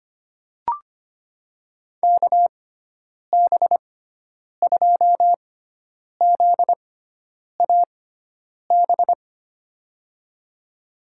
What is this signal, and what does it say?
Morse code, heard as a keyed tone, about 25 wpm, spaced as for 9 wpm, 700 Hz. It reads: KB2ZAB
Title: USA Call Signs - 6 Characters - 6x Character Spacing - 25wpm